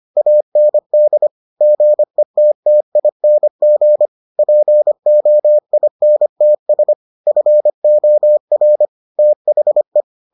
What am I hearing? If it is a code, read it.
Morse code, heard as a keyed tone, about 25 wpm, 600 Hz. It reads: AND GETTING POINTS FOR THE